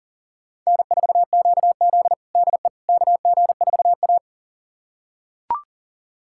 Morse code, heard as a keyed tone, about 40 wpm, 700 Hz. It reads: N4QZ DE KG4A
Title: USA Callsign DE Callsign - 4 Characters - 40wpm